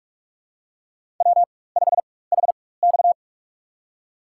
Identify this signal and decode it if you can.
Morse code, heard as a keyed tone, about 42 wpm, spaced as for 21 wpm, 700 Hz. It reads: W5HX
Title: USA Call Signs - 4 Characters - 4x Character Spacing - 45wpm